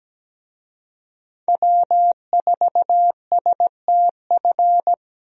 Morse code, heard as a keyed tone, about 17 wpm, 700 Hz. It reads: W4STF
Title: USA Call Signs - 5 Characters - Repeated 2x - 17wpm